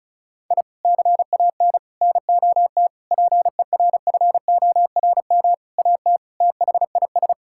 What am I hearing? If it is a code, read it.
Morse code, heard as a keyed tone, about 36 wpm, 700 Hz. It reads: I CAN NOT PERFORM AT THIS
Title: Sentences from Top 1000 Words - Review Part 5 - 35wpm